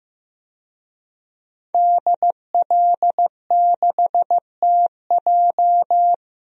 Morse code, heard as a keyed tone, about 15 wpm, 700 Hz. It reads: DL6TJ
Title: DXCC-100 Call Signs - 5 Characters - Repeated 3x - 15wpm